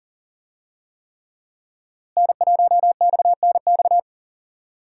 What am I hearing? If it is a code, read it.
Morse code, heard as a keyed tone, about 37 wpm, 700 Hz. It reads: N1XNX
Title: USA Call Signs - 5 Characters - Repeated 3x - 40wpm